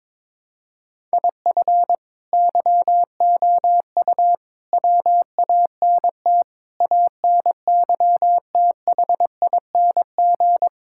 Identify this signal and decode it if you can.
Morse code, heard as a keyed tone, about 22 wpm, 700 Hz. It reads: IF YOU WANT ANYTHING
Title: Sentences from Top 1000 Words - Review Part 6 - 22wpm